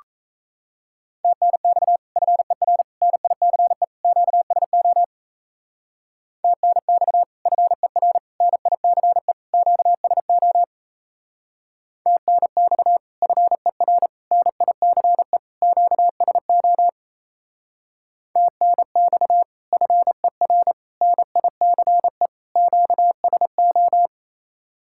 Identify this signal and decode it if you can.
Morse code, heard as a keyed tone, about 36 wpm, 700 Hz. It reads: TNX FER NICE QSO TNX FER NICE QSO TNX FER NICE QSO TNX FER NICE QSO